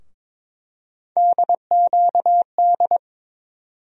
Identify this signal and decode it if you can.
Morse code, heard as a keyed tone, about 22 wpm, 700 Hz. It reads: DQD
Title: USA Call Sign Postfixes - 3 Letters 22wpm